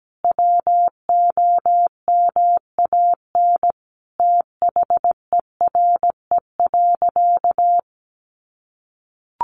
Morse code, heard as a keyed tone, about 17 wpm, 700 Hz. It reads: WOMAN THERE.
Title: Sentences from Top 1100 Words 17wpm